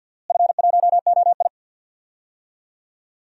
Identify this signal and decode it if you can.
Morse code, heard as a keyed tone, about 52 wpm, 700 Hz. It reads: F1OI
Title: DXCC-100 Call Signs - 4 Characters - Repeated 3x - 50wpm